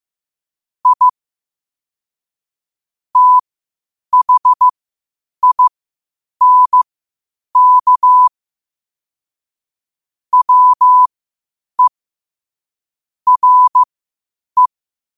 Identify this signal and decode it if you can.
Morse code, heard as a keyed tone, about 15 wpm, spaced as for 8 wpm, 1000 Hz. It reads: I THINK WE RE
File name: qotdfiles / 8wpm